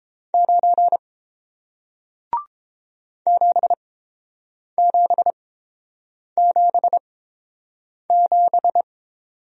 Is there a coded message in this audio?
Yes: Morse code from 0.3 to 8.8 s, heard as a keyed tone, about 30 wpm, spaced as for 7 wpm, 700 Hz. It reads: 9 7777